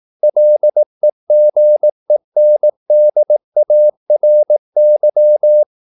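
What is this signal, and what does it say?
Morse code, heard as a keyed tone, about 18 wpm, 600 Hz. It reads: LEGENDARY
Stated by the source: Morse practice at 18 wpm, English text, from CW med Gnister.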